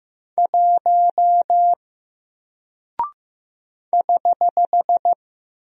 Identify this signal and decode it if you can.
Morse code, heard as a keyed tone, about 15 wpm, 700 Hz. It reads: 1 <HH>
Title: Single Character (Letter:Number:Punctuation:Prosign) Alt 15wpm